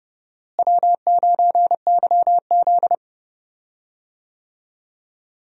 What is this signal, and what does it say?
Morse code, heard as a keyed tone, about 31 wpm, 700 Hz. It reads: W9YZ
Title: USA Call Signs - 4 Characters - Repeated 3x - 30wpm